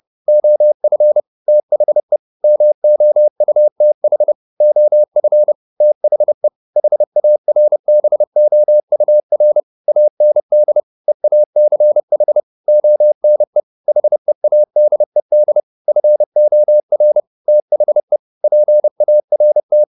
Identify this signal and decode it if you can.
Morse code, heard as a keyed tone, about 30 wpm, 600 Hz. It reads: OF THE MOUTH OF THE HARBOUR AND EACH ONE HEADED FOR THE PART
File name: the old man and the sea / the old man and the sea 1 30wpm